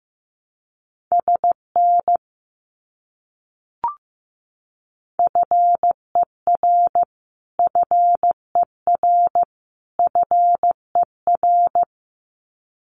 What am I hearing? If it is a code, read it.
Morse code, heard as a keyed tone, about 15 wpm, 700 Hz. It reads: SN FER FER FER
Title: Abbreviations Repeated 3x - 15wpm